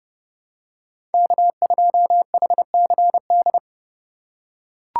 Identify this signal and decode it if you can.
Morse code, heard as a keyed tone, about 31 wpm, 700 Hz. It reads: K2HCD